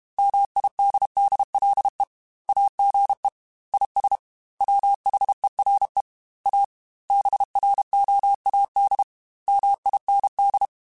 Morse code, heard as a keyed tone, about 32 wpm, 800 Hz. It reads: MIDDLE AGE IS WHERE A BROAD MIND